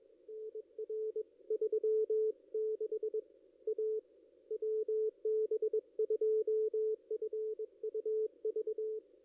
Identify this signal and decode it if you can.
Morse code, heard as a keyed tone, about 20 wpm, 440 Hz. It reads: NR 36 A WB2FUV
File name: ss-cw-k5tr-op-n5rz